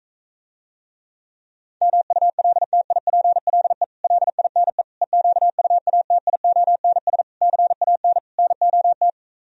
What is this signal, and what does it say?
Morse code, heard as a keyed tone, about 45 wpm, 700 Hz. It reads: MULTIPLE LINE EQUATIONS CAN NOT